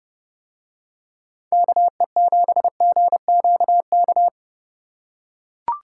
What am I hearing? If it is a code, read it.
Morse code, heard as a keyed tone, about 28 wpm, 700 Hz. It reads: KE7GQK